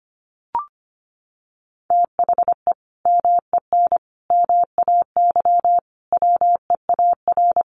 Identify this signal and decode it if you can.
Morse code, heard as a keyed tone, about 25 wpm, 700 Hz. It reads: THE MEN MAY WEAR
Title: Sentences from Top 1500 Words 25wpm